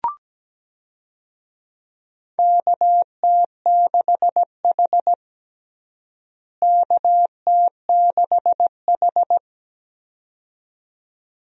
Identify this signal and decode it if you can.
Morse code, heard as a keyed tone, about 17 wpm, 700 Hz. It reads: KT6H KT6H
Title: USA Call Signs - 4 Characters - Repeated 2x - 17wpm